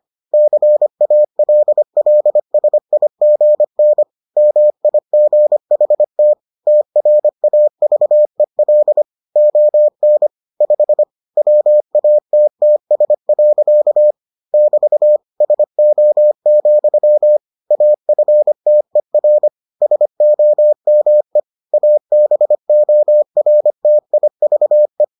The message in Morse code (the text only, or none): CALLSIGN MIGHT TRAVEL ON 5 WATTS. <BT> SO, AFTER SOME ABORTIVE